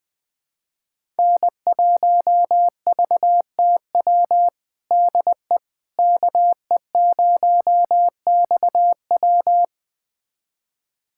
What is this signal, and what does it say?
Morse code, heard as a keyed tone, about 20 wpm, 700 Hz. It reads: N1VTW DE KE0XW
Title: USA Callsign DE Callsign - 5 Characters - 20wpm